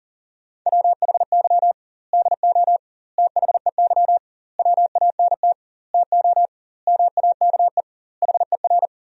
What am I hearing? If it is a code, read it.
Morse code, heard as a keyed tone, about 40 wpm, 700 Hz. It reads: WHY DO THEY WANT TO MAKE HER
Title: 2x Word-Spacing - Sentences from Top 400 Words - 40wpm